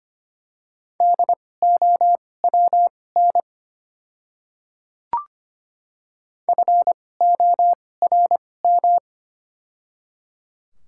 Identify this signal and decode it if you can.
Morse code, heard as a keyed tone, about 25 wpm, spaced as for 18 wpm, 700 Hz. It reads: DOWN FORM
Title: 2x Character Spacing - Top 300 Words - 25wpm